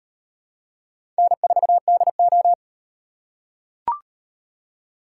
Morse code, heard as a keyed tone, about 38 wpm, 700 Hz. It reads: N4DO